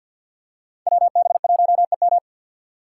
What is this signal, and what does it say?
Morse code, heard as a keyed tone, about 51 wpm, 700 Hz. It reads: WB1EM
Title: Code Groups - 5 Letter:Numbers - 50wpm